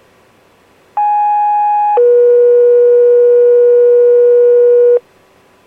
Two-tone page, heard as a one-second tone, then a three-second tone, about 830 then 480 Hz. It is heard from 1.0 to 5.0 s.